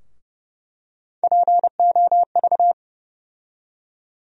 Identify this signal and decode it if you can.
Morse code, heard as a keyed tone, about 31 wpm, 700 Hz. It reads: POV
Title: USA Call Sign Postfixes - 3 Letters 30wpm